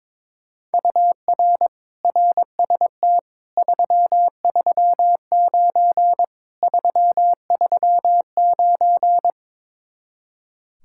Morse code, heard as a keyed tone, about 23 wpm, 700 Hz. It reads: UR RST 339 339
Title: Speed-Racing Common QSO Phrases - 22wpm